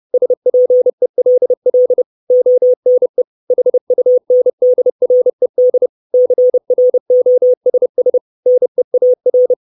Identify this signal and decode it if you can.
Morse code, heard as a keyed tone, about 30 wpm, 500 Hz. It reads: SPELL ONE HUNDRED CROSS NEAR
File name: 632 Words 30WPM